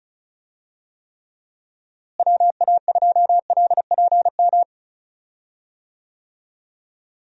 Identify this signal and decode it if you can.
Morse code, heard as a keyed tone, about 36 wpm, 700 Hz. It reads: WA2LPM
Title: USA Call Signs - 6 Characters - Repeated 3x - 35wpm